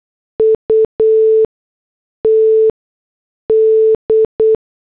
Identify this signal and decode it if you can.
Morse code, heard as a keyed tone, about 8 wpm, spaced as for 6 wpm, 440 Hz. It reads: UT D
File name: morse / headlines